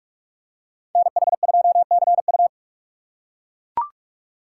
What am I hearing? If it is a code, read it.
Morse code, heard as a keyed tone, about 44 wpm, 700 Hz. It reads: NH2CU